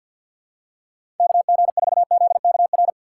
Morse code, heard as a keyed tone, about 52 wpm, 700 Hz. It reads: KG4ZKR
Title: USA Call Signs - 6 Characters - 8x Character Spacing - 50wpm